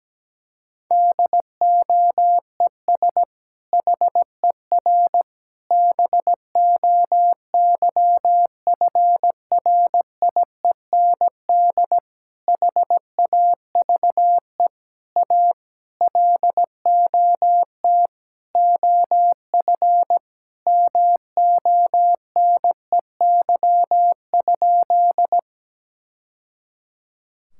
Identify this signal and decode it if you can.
Morse code, heard as a keyed tone, about 17 wpm, 700 Hz. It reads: DOES HER BOYFRIEND HAVE A LOT OF MONEY?
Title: Sentences from Top 1000 Words - Review Part 5 - 17wpm